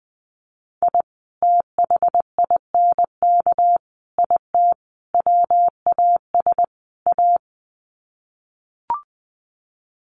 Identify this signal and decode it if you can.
Morse code, heard as a keyed tone, about 20 wpm, 700 Hz. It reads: I THINK IT WAS A
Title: Sets of 5 Words 20wpm